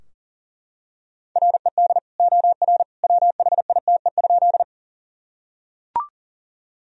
Morse code, heard as a keyed tone, about 37 wpm, 700 Hz. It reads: RED OR WHITE?
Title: Sentences from Top 1000 Words - Review Part 2 - 40wpm